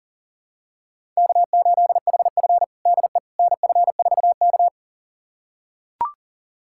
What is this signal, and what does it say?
Morse code, heard as a keyed tone, about 40 wpm, 700 Hz. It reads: K8HF DE NF4K
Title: USA Callsign DE Callsign - 4 Characters - 40wpm